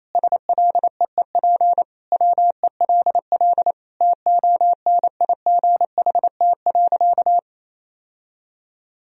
Morse code, heard as a keyed tone, about 28 wpm, 700 Hz. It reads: SLEEP WELL TONIGHT.
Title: Sentences from Top 1000 Words - Review Part 4 - 28wpm